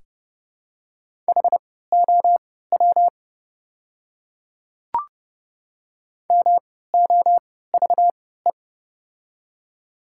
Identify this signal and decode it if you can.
Morse code, heard as a keyed tone, about 30 wpm, spaced as for 17 wpm, 690 Hz. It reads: HOW MOVE
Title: 3x Character Spacing - Top 200 Words - 30wpm